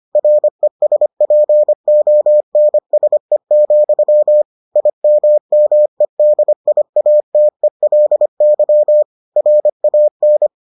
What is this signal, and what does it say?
Morse code, heard as a keyed tone, about 25 wpm, 600 Hz. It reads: RESPONSE, IMMEDIATELY RAN